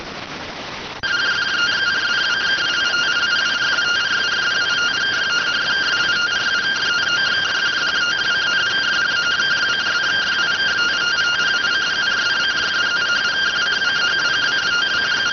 Two-tone teleprinter signal, about 170 Hz shift.